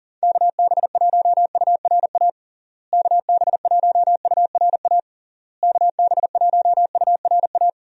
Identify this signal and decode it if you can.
Morse code, heard as a keyed tone, about 40 wpm, 700 Hz. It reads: KB1URA KB1URA KB1URA
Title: USA Call Signs - 6 Characters - Repeated 3x - 40wpm